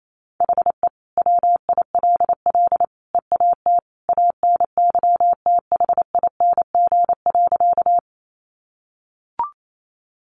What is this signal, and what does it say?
Morse code, heard as a keyed tone, about 28 wpm, 700 Hz. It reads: HE WILL EAT ANYTHING.
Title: Sentences from Top 1000 Words 28wpm